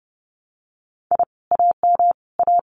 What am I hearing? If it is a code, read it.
Morse code, heard as a keyed tone, about 30 wpm, 700 Hz. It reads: I AM A
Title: Common QSO Phrases Encore 30wpm